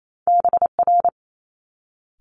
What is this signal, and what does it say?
Morse code, heard as a keyed tone, about 28 wpm, 700 Hz. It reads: BR
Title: Multi-letter Phonemes Quick Fire 28wpm